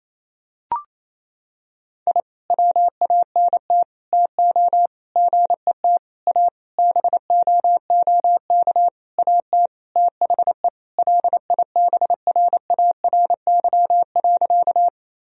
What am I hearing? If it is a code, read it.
Morse code, heard as a keyed tone, about 28 wpm, 700 Hz. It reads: I WANT TO GET A BOOK AT THE LIBRARY.
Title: Sentences from Top 1000 Words - Review Part 1 - 28wpm